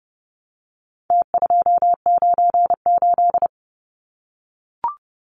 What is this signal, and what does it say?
Morse code, heard as a keyed tone, about 30 wpm, 700 Hz. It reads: T298
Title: Numbers - Contest Style - 4-Digit 30wpm